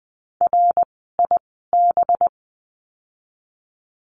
Morse code, heard as a keyed tone, about 20 wpm, spaced as for 15 wpm, 700 Hz. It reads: RIB
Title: 185 Lesson 30f - Letter b - Words 20wpm